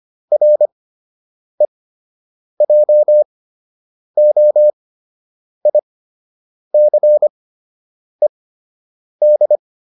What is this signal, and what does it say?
Morse code, heard as a keyed tone, about 25 wpm, spaced as for 8 wpm, 600 Hz. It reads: REJOICED